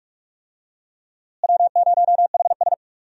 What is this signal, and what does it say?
Morse code, heard as a keyed tone, about 46 wpm, 700 Hz. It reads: W0HS
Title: USA Call Signs - 4 Characters - Repeated 2x - 45wpm